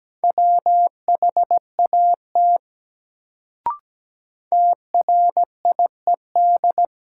Morse code, heard as a keyed tone, about 17 wpm, 700 Hz. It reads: WHAT TRIED
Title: Sets of 2 Words - New Word Drill - 17wpm